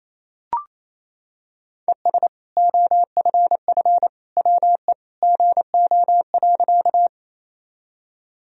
Morse code, heard as a keyed tone, about 28 wpm, 700 Hz. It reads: ES OFF WE GO.
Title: Sentences from Top 200 Words and Abbreviations 28wpm